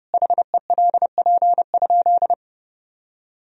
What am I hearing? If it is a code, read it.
Morse code, heard as a keyed tone, about 32 wpm, 700 Hz. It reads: HELP?
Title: Sentences from Top 200 Words and Abbreviations 30wpm